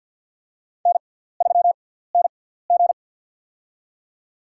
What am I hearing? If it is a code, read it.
Morse code, heard as a keyed tone, about 48 wpm, spaced as for 17 wpm, 700 Hz. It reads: N3NG
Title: USA Call Signs - 4 Characters - 6x Character Spacing - 50wpm